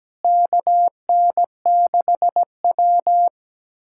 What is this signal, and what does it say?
Morse code, heard as a keyed tone, about 17 wpm, 700 Hz. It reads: KN6W
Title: USA Callsign DE Callsign - 4 Characters - 17wpm